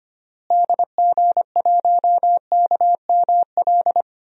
Morse code, heard as a keyed tone, about 25 wpm, 700 Hz. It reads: DG1KML
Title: DXCC-100 Call Signs - 6 Characters - Repeated 3x - 25wpm